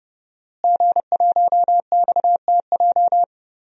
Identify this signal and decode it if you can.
Morse code, heard as a keyed tone, about 30 wpm, 700 Hz. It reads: G1XTJ
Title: DXCC-100 Call Signs - 5 Characters - Repeated 3x - 30wpm